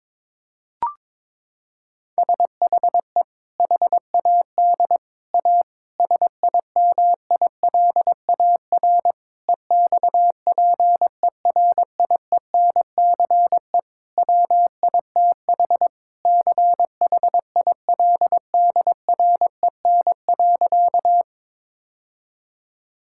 Morse code, heard as a keyed tone, about 22 wpm, 700 Hz. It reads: SHE HAD A SIMILAR EXPERIENCE WITH CHILDREN.
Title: Sentences from Top 1000 Words - Review Part 5 - 22wpm